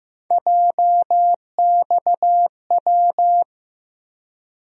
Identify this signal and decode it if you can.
Morse code, heard as a keyed tone, about 15 wpm, 700 Hz. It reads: JXW
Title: USA Call Sign Postfixes - 3 Letters 15wpm